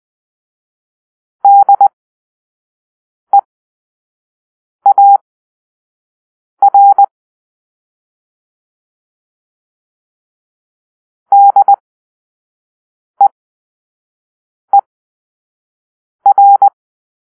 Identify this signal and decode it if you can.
Morse code, heard as a keyed tone, about 20 wpm, spaced as for 5 wpm, 800 Hz. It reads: DEAR DEER
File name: arrl / Grp-2 Drill-2